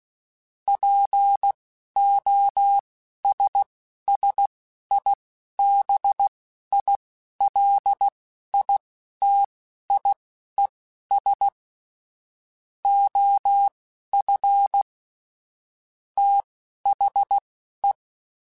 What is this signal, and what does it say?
Morse code, heard as a keyed tone, about 16 wpm, spaced as for 11 wpm, 800 Hz. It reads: POSSIBILITIES OF THE
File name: Technical Subjects / Technical Subjects-mp3s / OR Gate @10